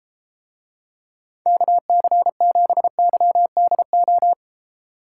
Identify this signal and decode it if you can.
Morse code, heard as a keyed tone, about 34 wpm, 700 Hz. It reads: KC7YDO